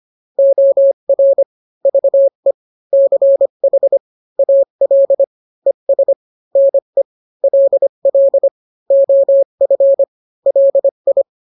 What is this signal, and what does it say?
Morse code, heard as a keyed tone, about 25 wpm, 550 Hz. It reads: OR VE CH AL ES NE LL OF LI